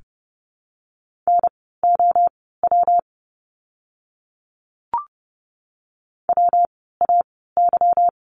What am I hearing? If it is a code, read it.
Morse code, heard as a keyed tone, about 30 wpm, spaced as for 17 wpm, 700 Hz. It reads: NOW WAY
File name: course / 30:10wpm / 230 Lesson 37f - Letter x - Words 30wpm